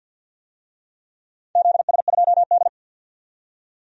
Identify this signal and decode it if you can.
Morse code, heard as a keyed tone, about 51 wpm, 700 Hz. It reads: ZS2D
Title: DXCC-100 Call Signs - 4 Characters - 50wpm